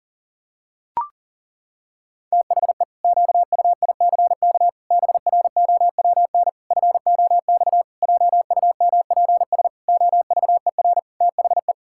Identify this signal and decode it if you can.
Morse code, heard as a keyed tone, about 40 wpm, 700 Hz. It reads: THE QUICK BROWN FOX JUMPS OVER THE
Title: Pangram Challenge v2 - 40wpm